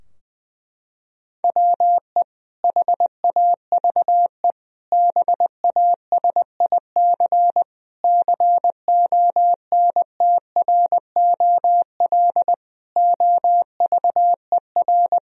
Morse code, heard as a keyed tone, about 20 wpm, 700 Hz. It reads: WE HAVE BASIC CONTROL OVER
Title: Sentences from Top 1000 Words - Review Part 3 - 20wpm